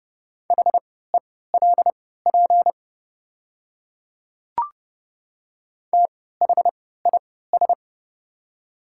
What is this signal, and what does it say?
Morse code, heard as a keyed tone, about 30 wpm, spaced as for 17 wpm, 700 Hz. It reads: HELP THIS
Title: 3x Character Spacing - Top 300 Words - 30wpm